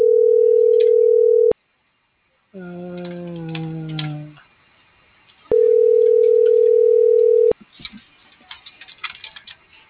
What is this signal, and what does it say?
Ringback tone (North America), 440 + 480 Hz.